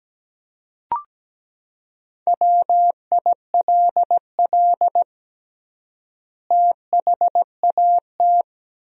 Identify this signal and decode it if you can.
Morse code, heard as a keyed tone, about 17 wpm, 700 Hz. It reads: WILL THAT
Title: 3x Word-Spacing - Sentences from Top 100 Words - 17wpm